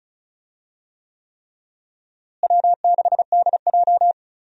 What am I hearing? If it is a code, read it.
Morse code, heard as a keyed tone, about 36 wpm, 700 Hz. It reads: W6DJ
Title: USA Call Signs - 4 Characters - Repeated 3x - 35wpm